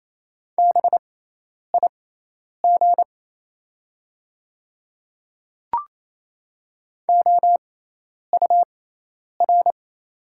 Morse code, heard as a keyed tone, about 28 wpm, spaced as for 10 wpm, 690 Hz. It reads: BIG OUR
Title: 6x Character Spacing - Top 300 Words - 28wpm